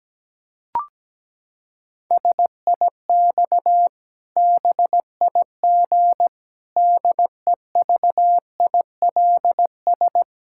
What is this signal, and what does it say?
Morse code, heard as a keyed tone, about 17 wpm, 700 Hz. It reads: SIX BIG DEVILS
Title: Pangram Challenge v2 - 17wpm